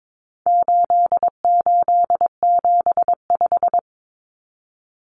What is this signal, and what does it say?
Morse code, heard as a keyed tone, about 22 wpm, 700 Hz. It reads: 8875
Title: Common Cut Numbers - 4-Digit 22wpm